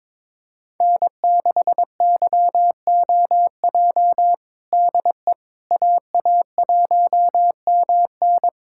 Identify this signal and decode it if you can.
Morse code, heard as a keyed tone, about 22 wpm, 700 Hz. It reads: N6YOJ DE AA1MN
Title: USA Callsign DE Callsign - 5 Characters - 22wpm